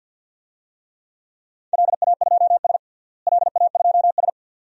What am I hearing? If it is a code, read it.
Morse code, heard as a keyed tone, about 51 wpm, 700 Hz. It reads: LA2S LA2S
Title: DXCC-100 Call Signs - 4 Characters - Repeated 2x - 50wpm